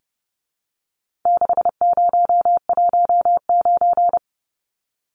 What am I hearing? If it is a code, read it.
Morse code, heard as a keyed tone, about 30 wpm, 700 Hz. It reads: 6019
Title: Numbers - 4-Digit 30wpm